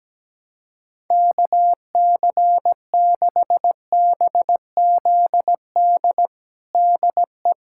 Morse code, heard as a keyed tone, about 17 wpm, 700 Hz. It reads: KC6BZD DE
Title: USA Callsign DE Callsign - 6 Characters - 17wpm